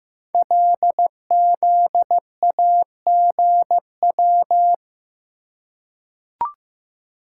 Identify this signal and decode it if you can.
Morse code, heard as a keyed tone, about 15 wpm, 700 Hz. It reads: LZAGW
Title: Code Groups - 5 Letter:Numbers - 15wpm